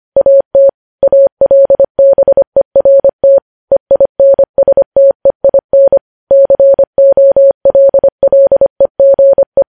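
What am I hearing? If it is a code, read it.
Morse code, heard as a keyed tone, about 25 wpm, 550 Hz. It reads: AT ALBERT EINSTEIN COLLEGE
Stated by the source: Morse code news headlines sent at 25 words per minute.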